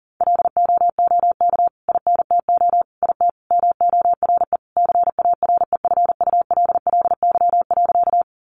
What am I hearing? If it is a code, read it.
Morse code, heard as a keyed tone, about 40 wpm, 700 Hz. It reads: LOOK INTO IT MORE CAREFULLY.
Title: Sentences from Top 1300 Words 40wpm